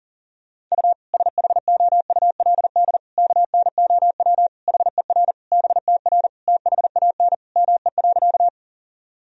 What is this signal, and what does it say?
Morse code, heard as a keyed tone, about 40 wpm, 700 Hz. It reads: U SHOULD KNOW HER BTR THAN ME.
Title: Sentences from Top 400 Words and Abbreviations 40wpm